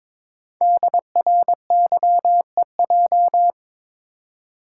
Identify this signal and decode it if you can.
Morse code, heard as a keyed tone, about 22 wpm, 700 Hz. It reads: DRYEJ
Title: Code Groups - 5 Letters - 22wpm